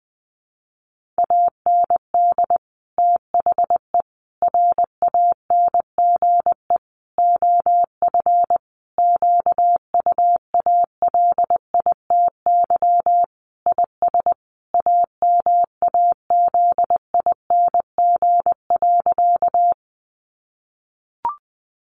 Morse code, heard as a keyed tone, about 20 wpm, 700 Hz. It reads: AND THE RANGE OF QUALITY IS AMAZING.
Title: Sentences from Top 1500 Words 20wpm